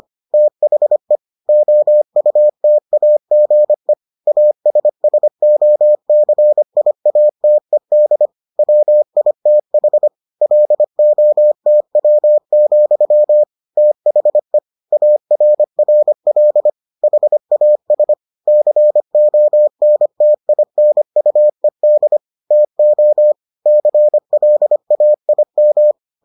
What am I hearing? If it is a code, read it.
Morse code, heard as a keyed tone, about 25 wpm, 600 Hz. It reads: THE OUTAGE ASSOCIATED WITH LOTW, THE ARRL HAS CONTINUED TO CLAIM